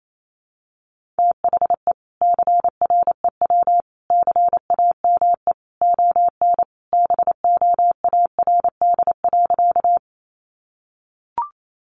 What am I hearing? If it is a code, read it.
Morse code, heard as a keyed tone, about 28 wpm, 700 Hz. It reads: THE CREW CAME ON BOARD.
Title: Sentences from Top 1500 Words 28wpm